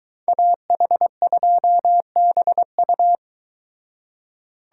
Morse code, heard as a keyed tone, about 23 wpm, 700 Hz. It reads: AH2BU